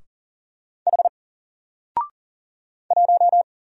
Morse code, heard as a keyed tone, about 39 wpm, 700 Hz. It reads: H 1
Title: Single Letter-Number Alt 40wpm